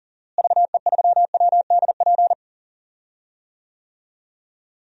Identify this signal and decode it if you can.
Morse code, heard as a keyed tone, about 39 wpm, 700 Hz. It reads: VE3WDP
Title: DXCC-100 Call Signs - 6 Characters - Repeated 3x - 40wpm